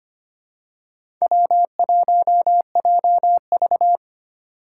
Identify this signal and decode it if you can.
Morse code, heard as a keyed tone, about 25 wpm, 700 Hz. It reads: W1JV
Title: USA Call Signs - 4 Characters - 4x Character Spacing - 25wpm